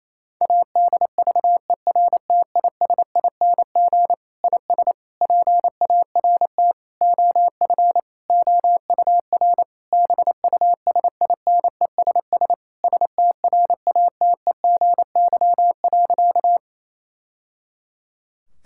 Morse code, heard as a keyed tone, about 28 wpm, 700 Hz. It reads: ADVERTISING IS PART OF OUR BUSINESS STRATEGY.
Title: Sentences from Top 1000 Words - Review Part 6 - 28wpm